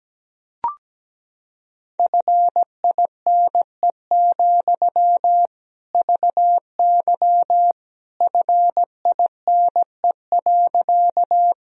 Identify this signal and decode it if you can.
Morse code, heard as a keyed tone, about 17 wpm, 700 Hz. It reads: FINE, VY FINE.
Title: Sentences from Top 500 Words and Abbreviations 17wpm